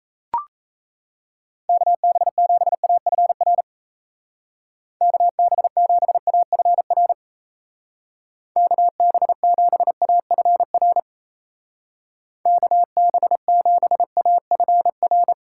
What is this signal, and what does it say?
Morse code, heard as a keyed tone, about 36 wpm, 700 Hz. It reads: KB7AFR KB7AFR KB7AFR KB7AFR